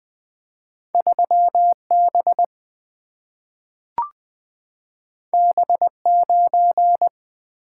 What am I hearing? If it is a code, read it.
Morse code, heard as a keyed tone, about 20 wpm, 700 Hz. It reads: 3B B9
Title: Single-Letter Single-Number 20wpm